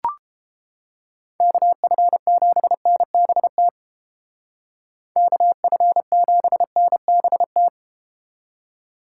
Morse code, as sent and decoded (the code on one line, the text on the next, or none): -.- ..-. --... -. -... - / -.- ..-. --... -. -... -
KF7NBT KF7NBT